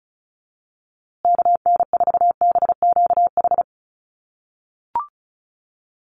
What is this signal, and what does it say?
Morse code, heard as a keyed tone, about 34 wpm, 700 Hz. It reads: KN4BQH
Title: USA Call Signs Repeated 2x 35wpm